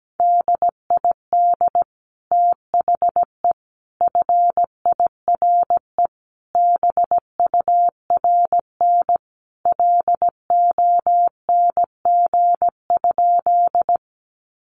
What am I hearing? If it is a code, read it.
Morse code, heard as a keyed tone, about 17 wpm, 700 Hz. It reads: DID THE FIRE BURN LONG?
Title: Sentences from Top 1300 Words 17wpm